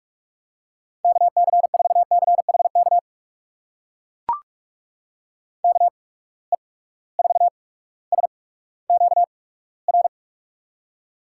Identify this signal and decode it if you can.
Morse code, heard as a keyed tone, about 44 wpm, spaced as for 19 wpm, 700 Hz. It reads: KC4CHK KE4SQR
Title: USA Call Signs - 6 Characters - 8x Character Spacing - 45wpm